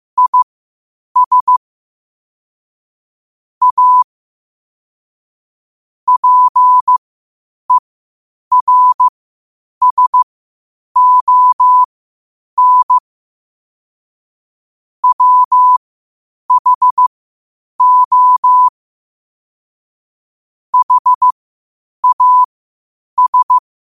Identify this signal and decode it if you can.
Morse code, heard as a keyed tone, about 15 wpm, spaced as for 8 wpm, 1000 Hz. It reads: IS A PERSON WHO HAS
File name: qotdfiles / 8wpm